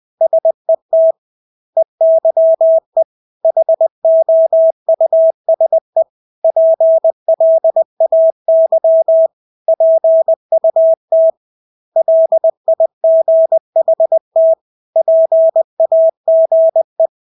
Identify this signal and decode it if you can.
Morse code, heard as a keyed tone, about 20 wpm, 650 Hz. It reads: SET EYE HOUSE PLAY PUT LIGHT PAGE
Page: Engelske ord 20 wpm